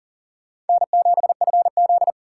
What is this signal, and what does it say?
Morse code, heard as a keyed tone, about 52 wpm, 700 Hz. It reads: N7FZ